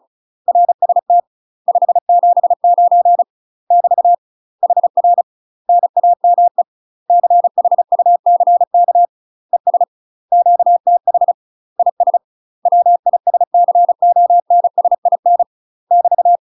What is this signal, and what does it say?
Morse code, heard as a keyed tone, about 34 wpm, 700 Hz. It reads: RST 579 <BT> HR NAME CHUCK ES QTH IS WISCONSIN <BT>
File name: UK MASTER QSOs TRAINING / 35WPM / UK MASTER QSOs 0003